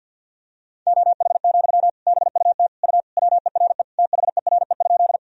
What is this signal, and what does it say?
Morse code, heard as a keyed tone, about 52 wpm, 700 Hz. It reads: OH, BUT U WERE THERE?
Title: Sentences from Top 500 Words and Abbreviations 50wpm